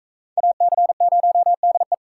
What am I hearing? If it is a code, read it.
Morse code, heard as a keyed tone, about 40 wpm, 700 Hz. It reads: AC0DE